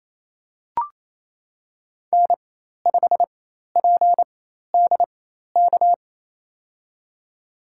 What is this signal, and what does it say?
Morse code, heard as a keyed tone, about 28 wpm, spaced as for 13 wpm, 700 Hz. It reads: N5PDK
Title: USA Call Signs - 5 Characters - 4x Character Spacing - 28wpm